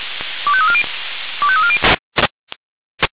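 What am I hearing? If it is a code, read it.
Five-tone selective call (ZVEI), heard as quick runs of five tones, tones of about 70 ms. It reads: 25340 25340
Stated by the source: ZVEI five-tone selective call received on SDR.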